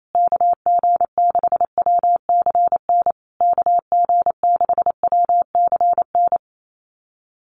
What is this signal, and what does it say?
Morse code, heard as a keyed tone, about 28 wpm, 700 Hz. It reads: KG6WCN KG6WCN
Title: USA Call Signs Repeated 3x 28wpm